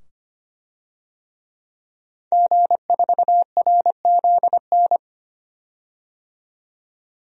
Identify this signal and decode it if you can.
Morse code, heard as a keyed tone, about 25 wpm, 700 Hz. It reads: G4RZN
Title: DXCC-100 Call Signs - 5 Characters - 25wpm